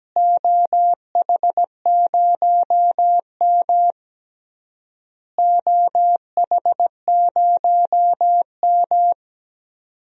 Morse code, heard as a keyed tone, about 17 wpm, 700 Hz. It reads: OH0M OH0M
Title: DXCC-100 Call Signs - 4 Characters - Repeated 3x - 17wpm